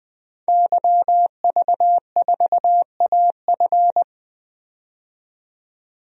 Morse code, heard as a keyed tone, about 20 wpm, 700 Hz. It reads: YV4AF
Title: DXCC-100 Call Signs - 5 Characters - 20wpm